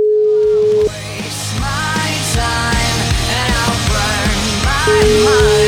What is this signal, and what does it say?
Ringback tone (Europe, 425 Hz), with noise.